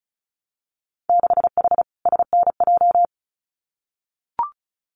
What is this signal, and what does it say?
Morse code, heard as a keyed tone, about 36 wpm, 700 Hz. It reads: 6H SNJ
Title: Winter Field Day Exchanges Encore 35wpm